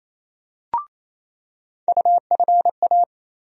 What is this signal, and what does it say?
Morse code, heard as a keyed tone, about 28 wpm, 700 Hz. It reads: UFA
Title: USA Call Sign Postfixes - 3 Letters 28wpm